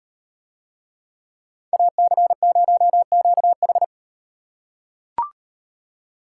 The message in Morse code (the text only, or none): AC0QH